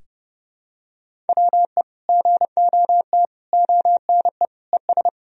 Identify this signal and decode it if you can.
Morse code, heard as a keyed tone, about 32 wpm, 700 Hz. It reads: WE GOT ONE ES
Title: Sentences from Top 300 Words and Abbreviations 30wpm